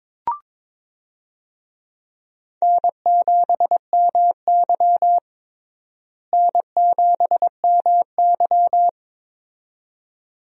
Morse code, heard as a keyed tone, about 22 wpm, 700 Hz. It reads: N7MY N7MY
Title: USA Call Signs - 4 Characters - Repeated 2x - 22wpm